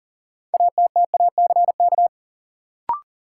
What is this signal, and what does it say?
Morse code, heard as a keyed tone, about 40 wpm, 700 Hz. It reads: ATTACK
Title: Sentences from Top 1500 Words - New Word Drill Alt - 40wpm